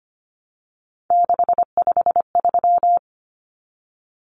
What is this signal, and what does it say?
Morse code, heard as a keyed tone, about 25 wpm, 700 Hz. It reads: 653
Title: Numbers - 3-Digit 25wpm